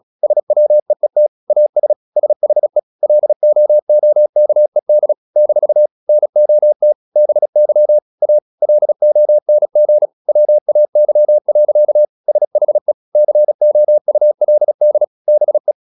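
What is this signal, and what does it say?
Morse code, heard as a keyed tone, about 36 wpm, 600 Hz. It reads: SWEET AS SHE LOOKED - NOT BY A LONG WAY. SHE COULD BE
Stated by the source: Morse practice at 36 wpm, text of Ken Follett's novel Never, from CW med Gnister.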